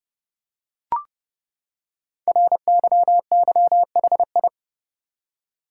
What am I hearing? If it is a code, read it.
Morse code, heard as a keyed tone, about 29 wpm, 690 Hz. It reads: RYYHI
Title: Code Groups - 5 Letters - 30wpm